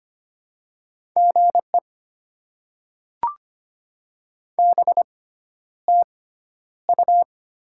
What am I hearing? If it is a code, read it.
Morse code, heard as a keyed tone, about 25 wpm, spaced as for 11 wpm, 700 Hz. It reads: GE BTU
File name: Top 20 QSO Elements - 6x Character Spacing - 25wpm